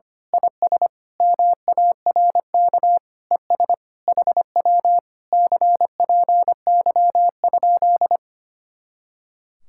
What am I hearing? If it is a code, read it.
Morse code, heard as a keyed tone, about 24 wpm, 700 Hz. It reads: IS MARK ES HW CPY?
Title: Speed-Racing Common QSO Phrases - 25wpm